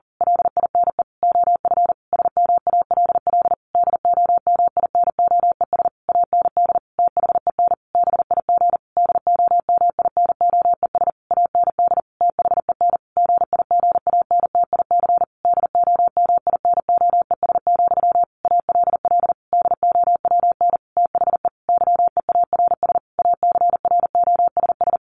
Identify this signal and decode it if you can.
Morse code, heard as a keyed tone, about 40 wpm, 700 Hz. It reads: LINE OF SMALL DOMINOES AND THEN BIG DOMINOES AND THEN GIGANTIC DOMINOES, ALL DOWN THE YEARS ACROSS
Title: A Sound Of Thunder 40wpm